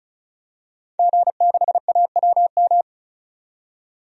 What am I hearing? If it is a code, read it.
Morse code, heard as a keyed tone, about 34 wpm, 700 Hz. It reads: G6AWM
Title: DXCC-100 Call Signs - 5 Characters - 35wpm